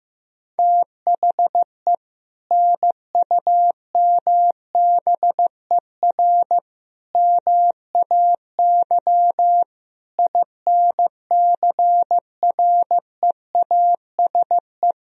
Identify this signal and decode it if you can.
Morse code, heard as a keyed tone, about 15 wpm, 700 Hz. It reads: THE NUMBER MAY INCREASE
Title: Sentences from Top 1000 Words - Review Part 4 - 15wpm